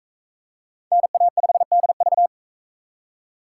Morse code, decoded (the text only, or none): NA5DV